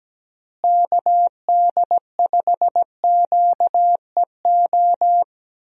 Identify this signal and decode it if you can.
Morse code, heard as a keyed tone, about 17 wpm, 700 Hz. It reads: KD5QEO
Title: USA Call Signs - 6 Characters - Repeated 3x - 17wpm